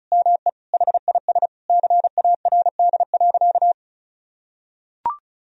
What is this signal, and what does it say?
Morse code, heard as a keyed tone, about 36 wpm, 700 Hz. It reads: ME HIS CARD.
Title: Sentences from Top 1000 Words - Review Part 4 - 35wpm